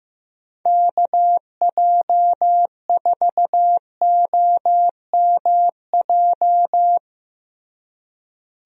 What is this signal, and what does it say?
Morse code, heard as a keyed tone, about 15 wpm, 700 Hz. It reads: KJ4OMJ
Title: USA Call Signs - 6 Characters - Repeated 2x - 15wpm